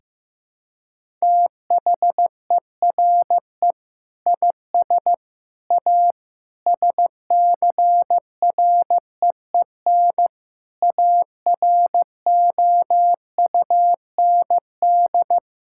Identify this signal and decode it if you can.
Morse code, heard as a keyed tone, about 15 wpm, 700 Hz. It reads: THERE IS A SCREEN AROUND
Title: Sentences from Top 1000 Words - Review Part 6 - 15wpm